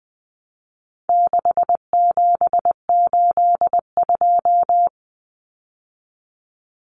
Morse code, heard as a keyed tone, about 20 wpm, 700 Hz. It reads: 6782
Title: Common Cut Numbers - 4-Digit 20wpm